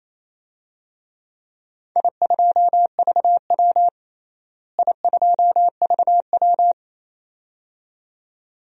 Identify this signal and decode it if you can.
Morse code, heard as a keyed tone, about 28 wpm, 700 Hz. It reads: I2VW I2VW
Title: DXCC-100 Call Signs - 4 Characters - Repeated 2x - 28wpm